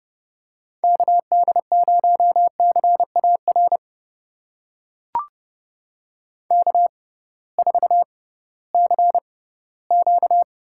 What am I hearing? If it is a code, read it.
Morse code, heard as a keyed tone, about 30 wpm, spaced as for 18 wpm, 700 Hz. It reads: KD0CAR K4CQ
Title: USA Call Signs - 6x Character Spacing - 30wpm